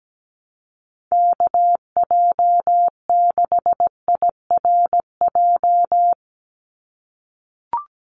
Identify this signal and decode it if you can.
Morse code, heard as a keyed tone, about 17 wpm, 700 Hz. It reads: KJ6IRJ
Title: USA Call Signs 17wpm